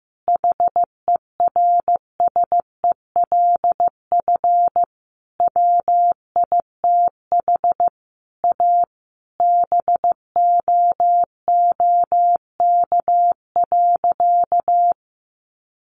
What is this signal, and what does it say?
Morse code, heard as a keyed tone, about 15 wpm, 700 Hz. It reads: HERSELF WITH A BOOK.
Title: Additional Easy Phrases 15wpm